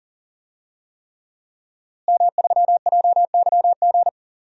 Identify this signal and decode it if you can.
Morse code, heard as a keyed tone, about 40 wpm, 700 Hz. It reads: M3JYG
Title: DXCC-100 Call Signs - 5 Characters - Repeated 3x - 40wpm